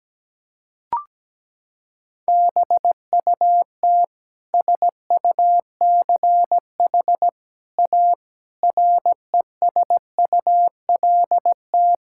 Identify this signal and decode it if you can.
Morse code, heard as a keyed tone, about 17 wpm, 700 Hz. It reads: BUT SUCH A RESULT
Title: Sentences from Top 1000 Words - Review Part 5 - 17wpm